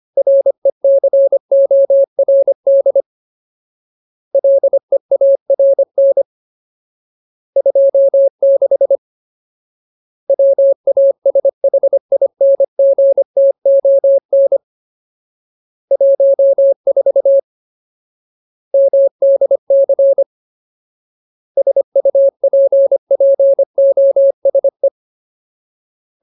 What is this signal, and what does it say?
Morse code, heard as a keyed tone, about 25 wpm, 550 Hz. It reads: RECORD LEARN 26 WASHINGTON 14 MDC SUPPOSE
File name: Ultimate Word List / 25WPM Ultimate